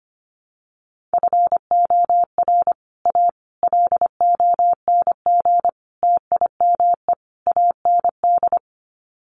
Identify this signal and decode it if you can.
Morse code, heard as a keyed tone, about 25 wpm, 700 Hz. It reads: FOR A LONG TIME AND
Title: Sets of 5 Words Encore2 25wpm